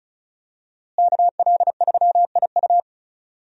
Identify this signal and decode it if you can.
Morse code, heard as a keyed tone, about 36 wpm, 700 Hz. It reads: KL3IU
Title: USA Call Signs - 5 Characters - Repeated 3x - 35wpm